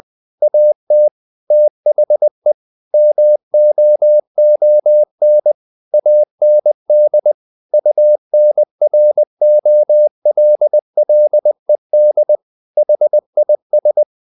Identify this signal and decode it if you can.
Morse code, heard as a keyed tone, about 20 wpm, 600 Hz. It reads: AT THE MOON AND UNROLLED HIS
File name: the old man and the sea / the old man and the sea part 1 20wpm